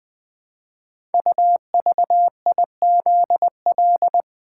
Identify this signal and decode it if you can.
Morse code, heard as a keyed tone, about 20 wpm, 700 Hz. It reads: UVIZL
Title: Code Groups - 5 Letters - 20wpm